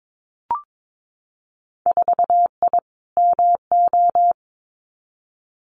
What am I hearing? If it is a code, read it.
Morse code, heard as a keyed tone, about 22 wpm, 700 Hz. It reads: 4I MO
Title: Winter Field Day Exchanges 22wpm